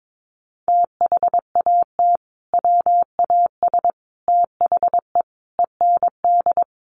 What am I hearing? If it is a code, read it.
Morse code, heard as a keyed tone, about 22 wpm, 700 Hz. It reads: THAT WAS THE END
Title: Sentences from Top 1400 Words 22wpm